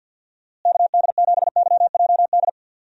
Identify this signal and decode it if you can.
Morse code, heard as a keyed tone, about 46 wpm, 700 Hz. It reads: KD7YJD
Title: USA Call Signs - 6 Characters - Repeated 2x - 50wpm